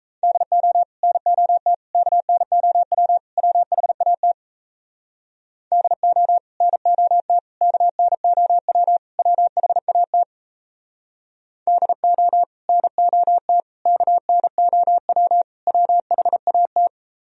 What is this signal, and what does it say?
Morse code, heard as a keyed tone, about 38 wpm, 700 Hz. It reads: DO NOT KNOW WHAT DO NOT KNOW WHAT DO NOT KNOW WHAT